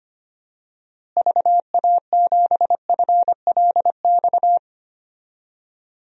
Morse code, heard as a keyed tone, about 25 wpm, 700 Hz. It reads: VA7FLX
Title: DXCC-100 Call Signs - 6 Characters - Repeated 2x - 25wpm